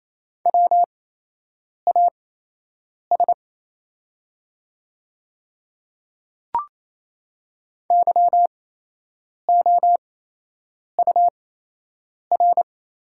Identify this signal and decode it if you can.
Morse code, heard as a keyed tone, about 27 wpm, spaced as for 8 wpm, 700 Hz. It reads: WAS YOUR
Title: 8x Character Spacing - Top 100 Words - 28wpm